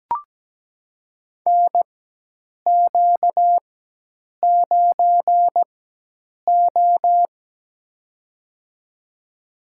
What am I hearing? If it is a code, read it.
Morse code, heard as a keyed tone, about 17 wpm, spaced as for 8 wpm, 700 Hz. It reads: NQ9O
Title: USA Call Signs - 4x Character Spacing - 17wpm